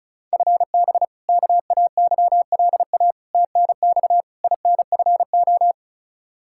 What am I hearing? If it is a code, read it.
Morse code, heard as a keyed tone, about 36 wpm, 700 Hz. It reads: FB KAYLA TNX INFO
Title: K1USN SST QSO Practice 35wpm